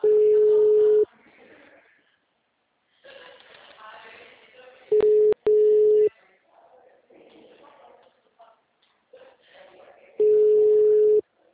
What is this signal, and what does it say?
Ringback tone (Europe, 425 Hz).